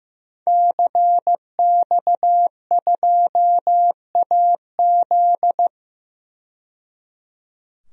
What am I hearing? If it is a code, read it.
Morse code, heard as a keyed tone, about 15 wpm, 700 Hz. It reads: CX2AZ
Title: DXCC-100 Call Signs - 5 Characters - 15wpm